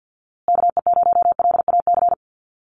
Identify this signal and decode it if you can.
Morse code, heard as a keyed tone, about 48 wpm, 700 Hz. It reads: KE0LAC
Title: USA Call Signs Repeated 3x 50wpm